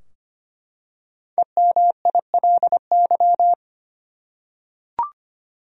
Morse code, heard as a keyed tone, about 25 wpm, 700 Hz. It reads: EMILY
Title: USA Common Names - 25wpm